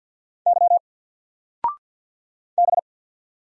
Morse code, heard as a keyed tone, about 49 wpm, 700 Hz. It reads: Y B
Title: Single Character (Letter:Number:Punctuation:Prosign) 50wpm